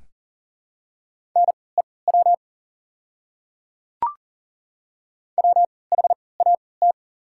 Morse code, heard as a keyed tone, about 40 wpm, spaced as for 23 wpm, 690 Hz. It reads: NEW WHAT
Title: 3x Character Spacing - Top 300 Words - 40wpm